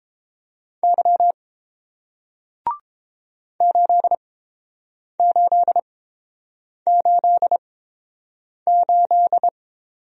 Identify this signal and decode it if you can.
Morse code, heard as a keyed tone, about 29 wpm, spaced as for 7 wpm, 700 Hz. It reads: Y 8888